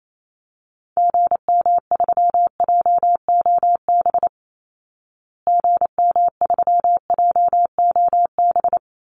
Morse code, heard as a keyed tone, about 28 wpm, 700 Hz. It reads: GM3JOB GM3JOB
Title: Call Signs Repeated 2x - 28wpm